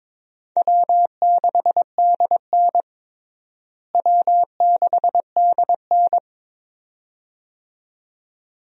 Morse code, heard as a keyed tone, about 22 wpm, 700 Hz. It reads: W6DN W6DN
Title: USA Call Signs - 4 Characters - Repeated 2x - 22wpm